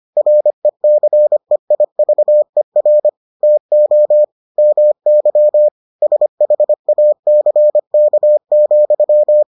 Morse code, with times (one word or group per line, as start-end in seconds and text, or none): 0.2-3.1 RECEIVER
3.4-4.2 TO
4.6-5.7 MY
6.0-9.4 SHACK,